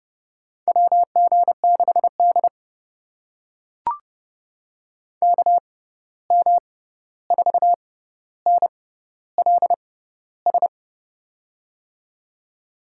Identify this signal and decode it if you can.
Morse code, heard as a keyed tone, about 32 wpm, spaced as for 14 wpm, 700 Hz. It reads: WG6D KM4NLS
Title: USA Call Signs - 6x Character Spacing - 30wpm